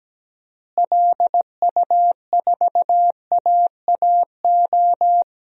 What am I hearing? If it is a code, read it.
Morse code, heard as a keyed tone, about 17 wpm, 700 Hz. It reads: LU4AAO
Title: DXCC-100 Call Signs - 6 Characters - Repeated 3x - 17wpm